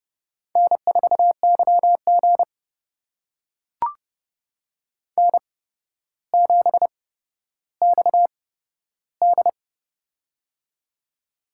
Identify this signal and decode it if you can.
Morse code, heard as a keyed tone, about 31 wpm, spaced as for 13 wpm, 700 Hz. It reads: N4YG N7XD
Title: USA Call Signs - 4 Characters - 8x Character Spacing - 30wpm